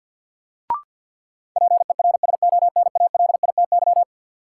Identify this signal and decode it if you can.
Morse code, heard as a keyed tone, about 53 wpm, 700 Hz. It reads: PERSONALITY
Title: Sentences from Top 1400 Words - New Word Drill - 50wpm